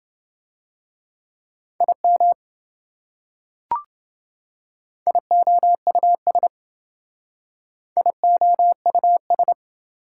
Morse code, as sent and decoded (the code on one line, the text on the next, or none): .. -- / .. --- ..- ... / .. --- ..- ...
IM IOUS IOUS